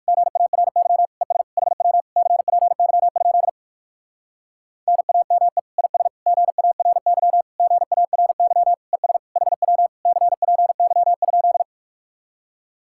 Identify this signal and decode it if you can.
Morse code, heard as a keyed tone, about 59 wpm, 700 Hz. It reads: GARY ES HW CPY? NAME IS GARY GARY ES HW CPY?